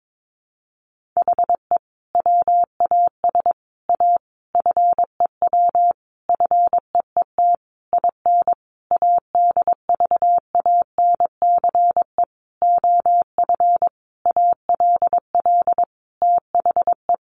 Morse code, heard as a keyed tone, about 22 wpm, 700 Hz. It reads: HE WAS A FEW FEET IN ADVANCE OF ALL THE
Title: Sentences from Top 1400 Words 22wpm